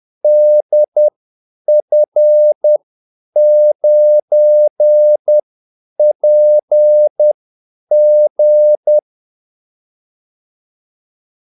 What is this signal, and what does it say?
Morse code, heard as a keyed tone, about 10 wpm, spaced as for 8 wpm, 600 Hz. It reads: DF9PG
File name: Callsigns-10-8wpm